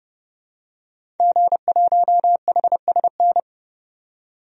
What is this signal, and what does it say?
Morse code, heard as a keyed tone, about 30 wpm, 700 Hz. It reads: G1HSN
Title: DXCC-100 Call Signs - 5 Characters - 30wpm